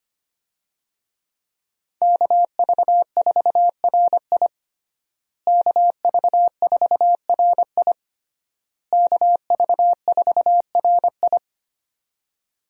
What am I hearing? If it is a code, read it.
Morse code, heard as a keyed tone, about 25 wpm, 700 Hz. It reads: KV4RI KV4RI KV4RI
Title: USA Call Signs - 5 Characters - Repeated 3x - 25wpm